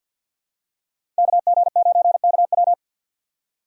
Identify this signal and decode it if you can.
Morse code, heard as a keyed tone, about 51 wpm, 700 Hz. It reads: KG9KW
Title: USA Call Signs - 5 Characters - 2x Character Spacing - 50wpm